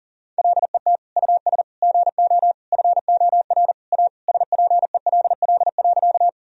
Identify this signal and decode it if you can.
Morse code, heard as a keyed tone, about 40 wpm, 700 Hz. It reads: LET US GO FOR A SPELL.
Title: Sentences from Top 1000 Words - Review Part 1 - 40wpm